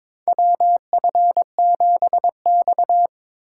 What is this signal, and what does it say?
Morse code, heard as a keyed tone, about 22 wpm, 700 Hz. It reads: WF7X
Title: USA Call Signs - 4 Characters - 2x Character Spacing - 22wpm